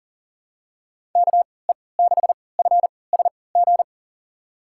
Morse code, heard as a keyed tone, about 39 wpm, spaced as for 22 wpm, 700 Hz. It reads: KE6FSG
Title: USA Call Signs - 3x Character Spacing - 40wpm